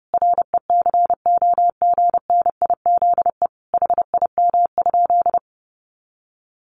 Morse code, heard as a keyed tone, about 30 wpm, 700 Hz. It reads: RECOGNIZE HIM?
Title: Sentences from Top 1400 Words 30wpm